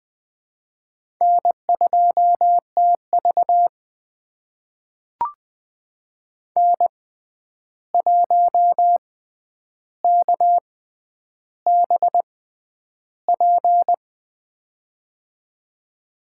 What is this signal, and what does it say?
Morse code, heard as a keyed tone, about 20 wpm, spaced as for 10 wpm, 700 Hz. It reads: N2TV N1KBP
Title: USA Call Signs - 6x Character Spacing - 20wpm